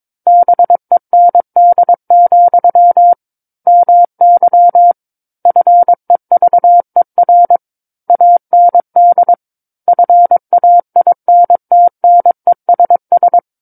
Morse code, heard as a keyed tone, about 22 wpm, 700 Hz. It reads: BEND, MY FEVER AND FAINTNESS
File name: Book One / 13-Chapter 13 - 22 WPM